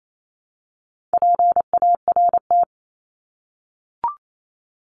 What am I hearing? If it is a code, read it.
Morse code, heard as a keyed tone, about 28 wpm, 700 Hz. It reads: PART
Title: ES:S Suffix - Top 100 Words with Base Words 28wpm